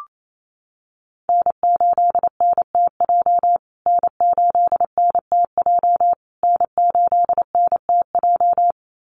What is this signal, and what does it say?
Morse code, heard as a keyed tone, about 28 wpm, 700 Hz. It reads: N8NTJ N8NTJ N8NTJ
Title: USA Call Signs Repeated 3x 28wpm